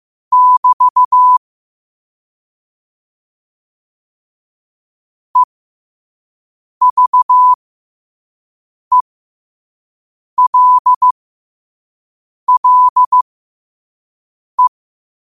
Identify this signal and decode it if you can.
Morse code, heard as a keyed tone, about 15 wpm, spaced as for 5 wpm, 1000 Hz. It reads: <BT> EVELLE
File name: qotdfiles / 5wpm